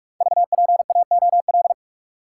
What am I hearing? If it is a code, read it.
Morse code, heard as a keyed tone, about 44 wpm, 700 Hz. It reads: VPAOL
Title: Code Groups - 5 Letters - 45wpm